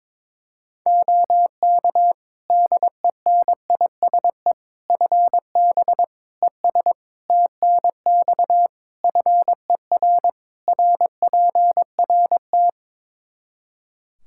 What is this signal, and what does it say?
Morse code, heard as a keyed tone, about 21 wpm, 700 Hz. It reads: OK DENISE FB ES TNX FER RPRT
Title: Practice QSO - Exchange 2 - Info - 22wpm